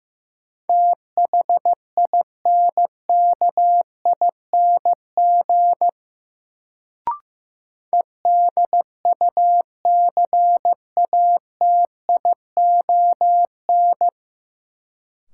Morse code, heard as a keyed tone, about 15 wpm, 700 Hz. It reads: THINKING EDUCATION
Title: Sentences from Top 1000 Words - Review Part 6 - New Word Drill - 15wpm